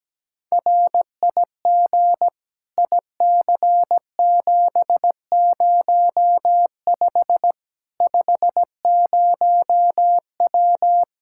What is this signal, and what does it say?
Morse code, heard as a keyed tone, about 17 wpm, 700 Hz. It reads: RIG IC705 50W
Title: Practice QSO - Exchange 2 - Info - 17wpm